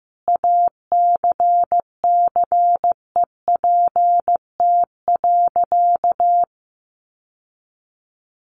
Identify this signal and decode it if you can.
Morse code, heard as a keyed tone, about 15 wpm, 700 Hz. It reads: ACCEPT.
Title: Sentences from Top 1300 Words 15wpm